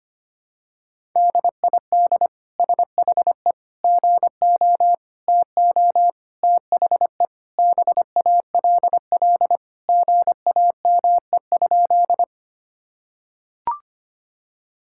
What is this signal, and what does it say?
Morse code, heard as a keyed tone, about 25 wpm, 700 Hz. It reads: DID SHE GO TO THE BALL GAME?
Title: Sentences from Top 1000 Words - Review Part 6 - 25wpm